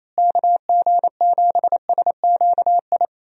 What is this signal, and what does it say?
Morse code, heard as a keyed tone, about 27 wpm, 700 Hz. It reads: KG7SQI
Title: USA Call Signs - 6 Characters - Repeated 3x - 28wpm